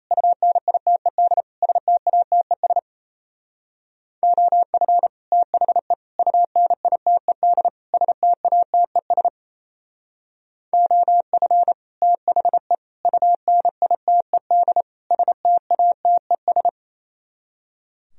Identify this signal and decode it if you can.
Morse code, heard as a keyed tone, about 37 wpm, 700 Hz. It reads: UNITED STATES OF THE UNITED STATES OF THE UNITED STATES